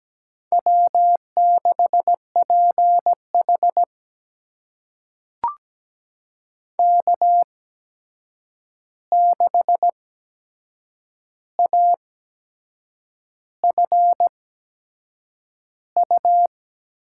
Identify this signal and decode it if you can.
Morse code, heard as a keyed tone, about 17 wpm, spaced as for 7 wpm, 700 Hz. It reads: W6PH K6AFU
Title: USA Call Signs - 8x Character Spacing - 17wpm